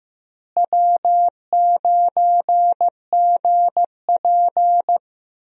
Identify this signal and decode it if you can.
Morse code, heard as a keyed tone, about 15 wpm, 700 Hz. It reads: W9GP
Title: USA Call Signs - 4 Characters - Repeated 2x - 15wpm